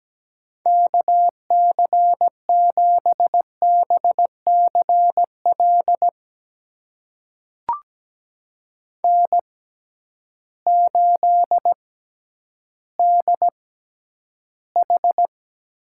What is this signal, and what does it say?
Morse code, heard as a keyed tone, about 17 wpm, spaced as for 10 wpm, 700 Hz. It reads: KC7BCL N8DH
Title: USA Call Signs - 6x Character Spacing - 17wpm